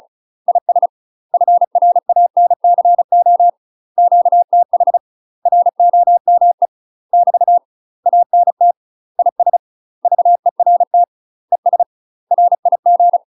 Morse code, heard as a keyed tone, about 36 wpm, 700 Hz. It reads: IS FRANCO QTH ROME <BT> ANT IS VERT ES RIG
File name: UK MASTER QSOs TRAINING / 35WPM / UK MASTER QSOs 0038